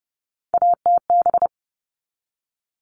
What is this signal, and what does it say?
Morse code, heard as a keyed tone, about 30 wpm, 700 Hz. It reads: ATB
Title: All Cut Numbers - 3-Digit 30wpm